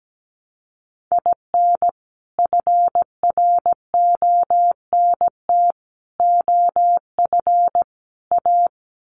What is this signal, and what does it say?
Morse code, heard as a keyed tone, about 17 wpm, 700 Hz. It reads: IN FRONT OF A
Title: Sets of 4 Words 17wpm